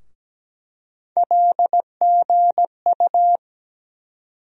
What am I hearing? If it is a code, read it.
Morse code, heard as a keyed tone, about 17 wpm, 700 Hz. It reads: LGU
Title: USA Call Sign Postfixes - 3 Letters 17wpm